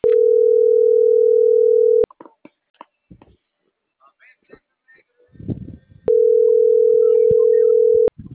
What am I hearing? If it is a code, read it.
Ringback tone (North America), 440 + 480 Hz.